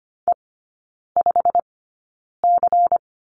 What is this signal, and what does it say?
Morse code, heard as a keyed tone, about 25 wpm, spaced as for 9 wpm, 700 Hz. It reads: E5C
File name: course / 25:10wpm / 093 Lesson 16e - Letter c - Three Characters 25wpm